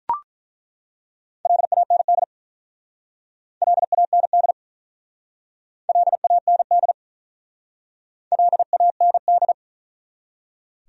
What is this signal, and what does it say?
Morse code, heard as a keyed tone, about 47 wpm, 700 Hz. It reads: LAND LAND LAND LAND